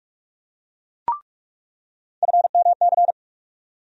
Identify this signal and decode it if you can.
Morse code, heard as a keyed tone, about 47 wpm, 700 Hz. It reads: FMC
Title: USA Call Sign Postfixes - 3 Letters 45wpm